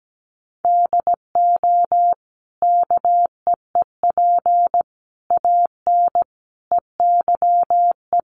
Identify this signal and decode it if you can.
Morse code, heard as a keyed tone, about 17 wpm, 700 Hz. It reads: DO KEEP AN EYE
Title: Sentences from Top 1100 Words 17wpm